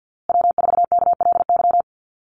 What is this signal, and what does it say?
Morse code, heard as a keyed tone, about 50 wpm, 700 Hz. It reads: W4KLY
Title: Call Signs Repeated 2x - 50wpm